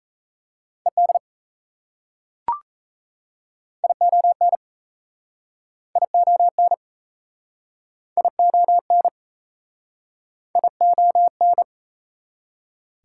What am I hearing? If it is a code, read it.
Morse code, heard as a keyed tone, about 38 wpm, 690 Hz. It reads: ED ION ION ION ION